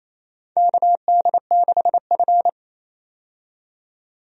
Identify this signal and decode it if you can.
Morse code, heard as a keyed tone, about 27 wpm, 700 Hz. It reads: KD6F
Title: USA Call Signs - 4 Characters - 28wpm